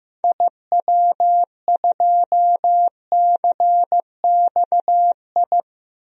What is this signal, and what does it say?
Morse code, heard as a keyed tone, about 15 wpm, 700 Hz. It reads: IW2CXI
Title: DXCC-100 Call Signs - 6 Characters - 15wpm